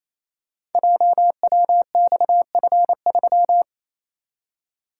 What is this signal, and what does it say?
Morse code, heard as a keyed tone, about 29 wpm, 700 Hz. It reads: JWXF3
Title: Code Groups - 5 Letter:Numbers - 28wpm